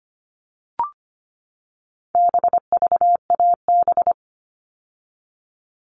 Morse code, heard as a keyed tone, about 25 wpm, 700 Hz. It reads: BVAB
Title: All Cut Numbers - 4-Digit 25wpm